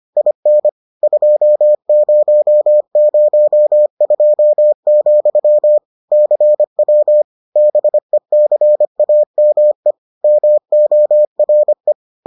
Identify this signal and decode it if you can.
Morse code, heard as a keyed tone, about 25 wpm, 600 Hz. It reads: IN 2002, CW BECAME MORE